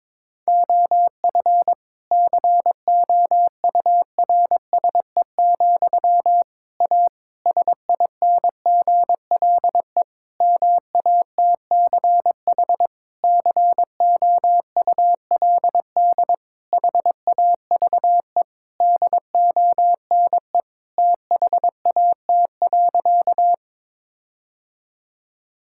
Morse code, heard as a keyed tone, about 22 wpm, 700 Hz. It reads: OF COURSE, A SINGLE MATCH COULD HAVE DONE THAT.
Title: Sentences from Top 1600 Words 22wpm